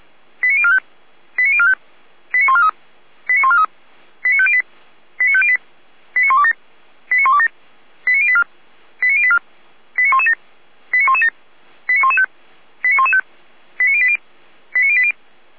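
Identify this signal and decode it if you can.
Five-tone selective call (ZVEI), heard as quick runs of five tones, tones of about 70 ms. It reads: 89045 89045 89143 89143 89508 89508 89127 89127 89084 89084 89117 89117 89115 89115 89080 89080
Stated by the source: ZVEI five-tone selective call 89092, received with an SDR.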